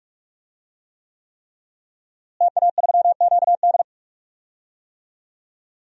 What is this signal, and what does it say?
Morse code, heard as a keyed tone, about 47 wpm, 700 Hz. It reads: TA3QD
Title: DXCC-100 Call Signs - 5 Characters - Repeated 3x - 45wpm